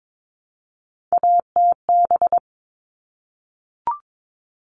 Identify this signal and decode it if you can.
Morse code, heard as a keyed tone, about 22 wpm, 700 Hz. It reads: ATB
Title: All Cut Numbers - 3-Digit 22wpm